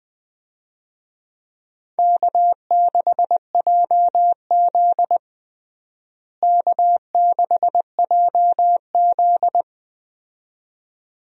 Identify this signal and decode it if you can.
Morse code, heard as a keyed tone, about 20 wpm, 700 Hz. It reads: K6JZ K6JZ
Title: USA Call Signs - 4 Characters - Repeated 2x - 20wpm